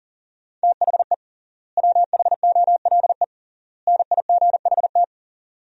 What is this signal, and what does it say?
Morse code, heard as a keyed tone, about 40 wpm, 700 Hz. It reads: THE WHOLE NIGHT
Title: 3x Word-Spacing - Sentences from Top 500 Words - 40wpm